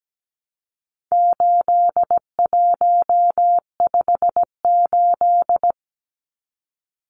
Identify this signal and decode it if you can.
Morse code, heard as a keyed tone, about 17 wpm, 700 Hz. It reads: 8158
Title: Numbers - 4-Digit 17wpm